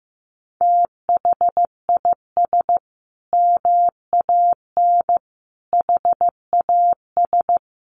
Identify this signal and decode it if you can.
Morse code, heard as a keyed tone, about 15 wpm, 700 Hz. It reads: THIS MAN HAS
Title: Sentences from Top 1300 Words 15wpm